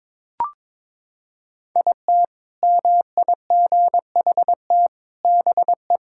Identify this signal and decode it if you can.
Morse code, heard as a keyed tone, about 22 wpm, 700 Hz. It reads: IT MIGHT BE
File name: practice / 22wpm / Sentences with 4-Word Phrases - 22wpm